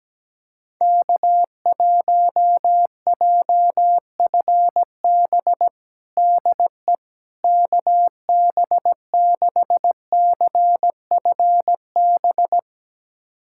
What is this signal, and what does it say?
Morse code, heard as a keyed tone, about 17 wpm, 700 Hz. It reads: K1JFB DE KB6CFB
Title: USA Callsign DE Callsign 17wpm